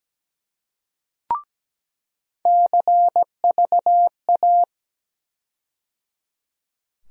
Morse code, heard as a keyed tone, about 17 wpm, 700 Hz. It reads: CVA
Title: USA Call Sign Postfixes - 3 Letters 17wpm